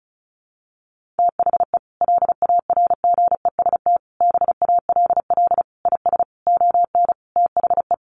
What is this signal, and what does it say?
Morse code, heard as a keyed tone, about 34 wpm, 700 Hz. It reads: THE LARGEST BALL IS ON THE
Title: Easy Phrases 35wpm